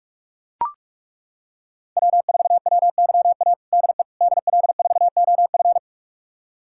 Morse code, heard as a keyed tone, about 41 wpm, 700 Hz. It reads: W4WYA DE DL4OF
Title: USA Callsign DE Callsign - 5 Characters - 45wpm